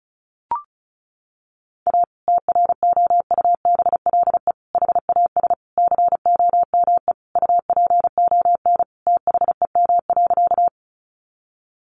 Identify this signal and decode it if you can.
Morse code, heard as a keyed tone, about 34 wpm, 700 Hz. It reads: A TROUBLE HAS COME UPON THEM.
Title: Sentences from Top 1200 Words 35wpm